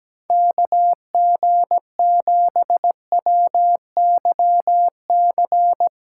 Morse code, heard as a keyed tone, about 17 wpm, 700 Hz. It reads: KG7WYC
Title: USA Callsign DE Callsign 17wpm